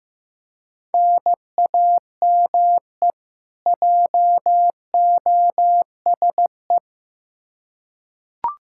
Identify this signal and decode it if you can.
Morse code, heard as a keyed tone, about 15 wpm, 700 Hz. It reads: NAME JOSE
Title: Practice QSO - Exchange 1 - Beginning - 15wpm